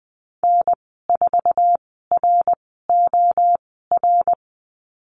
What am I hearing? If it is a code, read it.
Morse code, heard as a keyed tone, about 20 wpm, spaced as for 14 wpm, 700 Hz. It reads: N4ROR
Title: 055 Lesson 10g - Letter r - Call Signs 20wpm